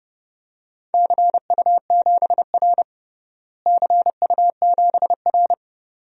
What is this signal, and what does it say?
Morse code, heard as a keyed tone, about 30 wpm, 700 Hz. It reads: CU7R CU7R
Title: DXCC-100 Call Signs - 4 Characters - Repeated 2x - 30wpm